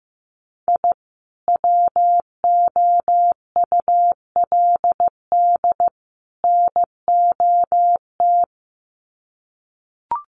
Morse code, heard as a keyed tone, about 15 wpm, 700 Hz. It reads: I WOULD NOT
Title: Sets of 3 Words 15wpm